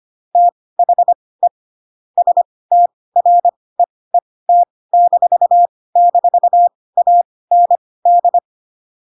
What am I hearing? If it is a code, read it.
Morse code, heard as a keyed tone, about 25 wpm, spaced as for 18 wpm, 700 Hz. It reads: THE STREET--AND
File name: Pinocchio / Pinocchio ch1 12 25 ik1hge